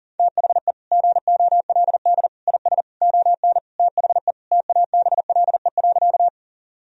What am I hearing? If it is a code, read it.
Morse code, heard as a keyed tone, about 40 wpm, 700 Hz. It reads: THE GOLD IS ON THE TABLE.
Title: Sentences from Top 1000 Words - Review Part 3 - 40wpm